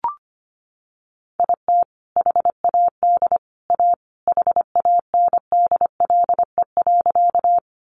Morse code, heard as a keyed tone, about 25 wpm, 700 Hz. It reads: IT HAD A HANDLE.
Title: Sentences from Top 1100 Words 25wpm